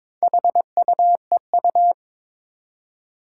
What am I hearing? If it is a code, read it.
Morse code, heard as a keyed tone, about 22 wpm, 700 Hz. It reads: HUEU
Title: Four Challenging Letter-Number Combinations 22wpm